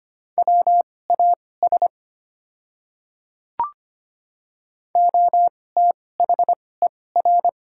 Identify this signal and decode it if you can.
Morse code, heard as a keyed tone, about 25 wpm, spaced as for 18 wpm, 690 Hz. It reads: WAS OTHER
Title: 2x Character Spacing - Top 100 Words - 25wpm